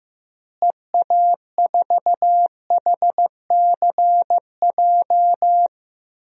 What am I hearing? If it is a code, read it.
Morse code, heard as a keyed tone, about 15 wpm, 700 Hz. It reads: EA4HCJ
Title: DXCC-100 Call Signs - 6 Characters - Repeated 2x - 15wpm